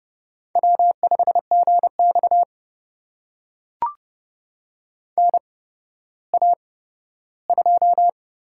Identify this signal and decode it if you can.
Morse code, heard as a keyed tone, about 30 wpm, spaced as for 15 wpm, 700 Hz. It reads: W5GX NA2
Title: USA Call Signs - 4 Characters - 8x Character Spacing - 30wpm